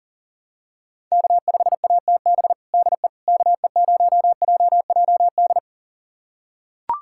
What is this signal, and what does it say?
Morse code, heard as a keyed tone, about 40 wpm, 700 Hz. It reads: K5ATB DE KE0JJD
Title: USA Callsign DE Callsign 40wpm